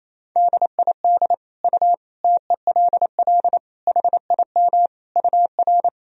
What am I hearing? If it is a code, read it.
Morse code, heard as a keyed tone, about 28 wpm, 700 Hz. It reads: DID U TELL HIM UR
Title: Sentences from Top 400 Words and Abbreviations 28wpm